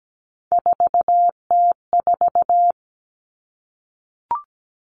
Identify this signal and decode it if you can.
Morse code, heard as a keyed tone, about 17 wpm, 700 Hz. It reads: 4T4
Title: Common Cut Numbers - 3-Digit 17wpm